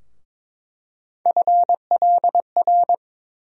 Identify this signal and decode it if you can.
Morse code, heard as a keyed tone, about 22 wpm, 700 Hz. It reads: FLR
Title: USA Call Sign Postfixes - 3 Letters 22wpm